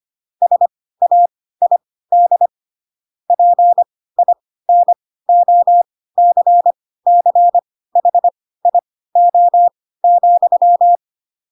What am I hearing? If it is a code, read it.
Morse code, heard as a keyed tone, about 25 wpm, spaced as for 16 wpm, 700 Hz. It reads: SAID PINOCCHIO,
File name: Pinocchio / Pinocchio ch7 10 25 ik1hge